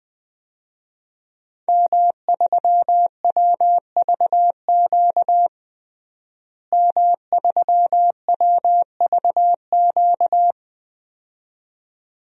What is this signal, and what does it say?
Morse code, heard as a keyed tone, about 20 wpm, 700 Hz. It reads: M3WVQ M3WVQ
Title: DXCC-100 Call Signs - 5 Characters - Repeated 2x - 20wpm